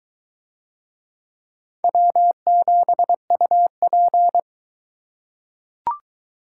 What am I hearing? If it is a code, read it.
Morse code, heard as a keyed tone, about 21 wpm, 700 Hz. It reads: W7UP